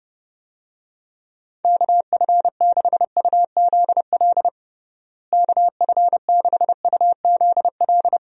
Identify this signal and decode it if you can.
Morse code, heard as a keyed tone, about 30 wpm, 700 Hz. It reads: KF6UZL KF6UZL
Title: USA Call Signs - 6 Characters - Repeated 2x - 30wpm